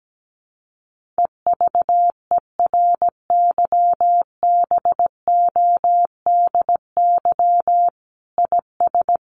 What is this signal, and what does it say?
Morse code, heard as a keyed tone, about 17 wpm, 700 Hz. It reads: EVERYBODY IS
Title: Sentences from Top 1400 Words 17wpm